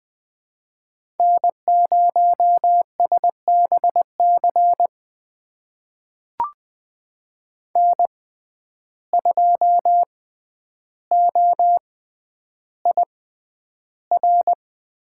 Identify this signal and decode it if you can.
Morse code, heard as a keyed tone, about 20 wpm, spaced as for 10 wpm, 700 Hz. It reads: N0SBC N2OIR
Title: USA Call Signs - 5 Characters - 6x Character Spacing - 20wpm